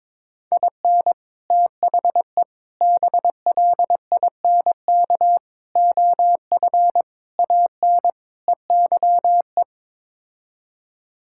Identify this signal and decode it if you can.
Morse code, heard as a keyed tone, about 22 wpm, 700 Hz. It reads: IN THE BLINK OF AN EYE
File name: practice / 22wpm / Sets of 6 Words - 22wpm